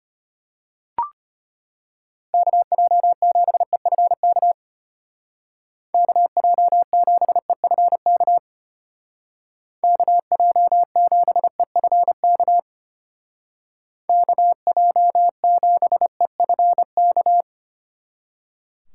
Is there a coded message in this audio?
Yes: Morse code from 2.3 to 17.4 s, heard as a keyed tone, about 32 wpm, 700 Hz. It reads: KJ7EFK KJ7EFK KJ7EFK KJ7EFK